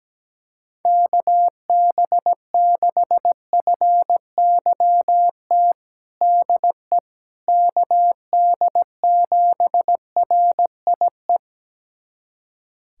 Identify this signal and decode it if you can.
Morse code, heard as a keyed tone, about 17 wpm, 700 Hz. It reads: KB6FYT DE KD7RIE
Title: USA Callsign DE Callsign - 6 Characters - 17wpm